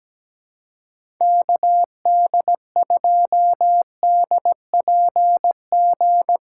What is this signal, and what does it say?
Morse code, heard as a keyed tone, about 17 wpm, 700 Hz. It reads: KD2DPG
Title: USA Call Signs - 6 Characters - 3x Character Spacing - 17wpm